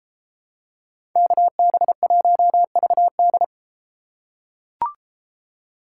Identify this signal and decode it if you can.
Morse code, heard as a keyed tone, about 34 wpm, 700 Hz. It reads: KB1VD